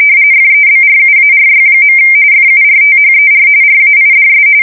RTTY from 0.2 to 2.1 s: FEBRUARY.